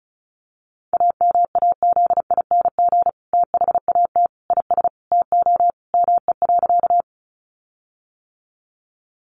Morse code, heard as a keyed tone, about 34 wpm, 700 Hz. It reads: AMAZING THAT IS TO ME.
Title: Sentences from Top 1500 Words 35wpm